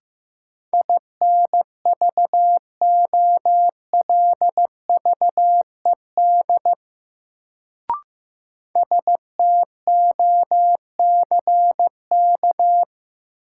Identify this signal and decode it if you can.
Morse code, heard as a keyed tone, about 15 wpm, 700 Hz. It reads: INVOLVED STOCK
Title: Sentences from Top 800 Words - New Word Drill - 15wpm